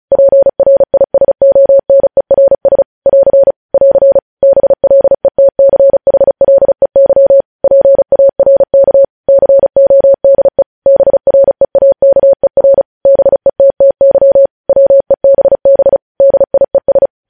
Morse code, heard as a keyed tone, about 36 wpm, 550 Hz. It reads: PRISONERS <AR> <AR> BLETCHLEY PARK CODE BREAKER BETTY WEBB DIES
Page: news headlines in morse code at 35 wpm